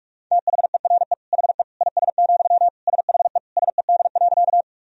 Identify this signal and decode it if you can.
Morse code, heard as a keyed tone, about 46 wpm, 700 Hz. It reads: THERE HE IS, SHE SED.
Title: Sentences from Top 100 Words and Abbreviations 45wpm